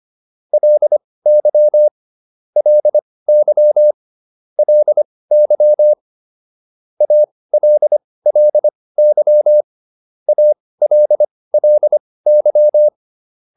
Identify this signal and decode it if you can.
Morse code, heard as a keyed tone, about 25 wpm, spaced as for 18 wpm, 600 Hz. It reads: LY LY LY ALLY ALLY